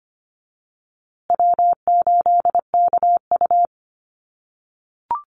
Morse code, heard as a keyed tone, about 25 wpm, 700 Hz. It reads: W8KU
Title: USA Call Signs Repeated 2x 25wpm